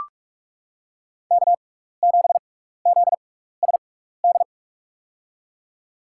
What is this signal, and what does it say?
Morse code, heard as a keyed tone, about 44 wpm, spaced as for 15 wpm, 700 Hz. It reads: K7ZSD
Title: USA Call Signs - 5 Characters - 6x Character Spacing - 45wpm